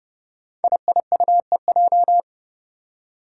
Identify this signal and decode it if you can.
Morse code, heard as a keyed tone, about 30 wpm, 700 Hz. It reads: IIUEJ
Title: Five Challenging Letter-Number Combinations 30wpm